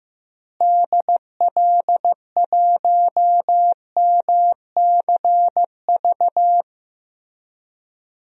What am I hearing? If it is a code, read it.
Morse code, heard as a keyed tone, about 15 wpm, 700 Hz. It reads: DL1MCV
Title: DXCC-100 Call Signs - 6 Characters - Repeated 3x - 15wpm